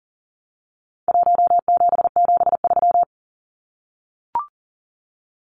Morse code, heard as a keyed tone, about 40 wpm, 700 Hz. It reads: 1773
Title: Numbers - Contest Style - 4-Digit 40wpm